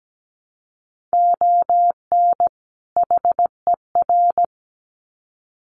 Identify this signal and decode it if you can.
Morse code, heard as a keyed tone, about 17 wpm, 700 Hz. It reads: ON HER
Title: Sets of 2 Words 17wpm